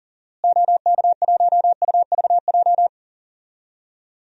Morse code, heard as a keyed tone, about 40 wpm, 700 Hz. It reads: OK1UVJ
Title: DXCC-100 Call Signs - 6 Characters - Repeated 2x - 40wpm